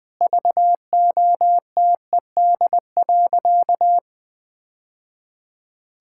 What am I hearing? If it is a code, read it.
Morse code, heard as a keyed tone, about 20 wpm, 700 Hz. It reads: VOTED.
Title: Sentences with 3-Word Phrases - 20wpm